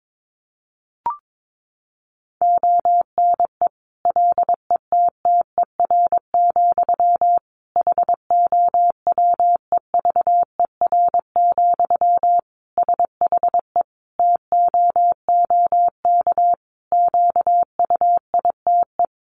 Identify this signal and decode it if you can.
Morse code, heard as a keyed tone, about 22 wpm, 700 Hz. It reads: ONE LETTER, HOWEVER, SHE TOOK QUITE
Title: Sentences from Top 1200 Words 22wpm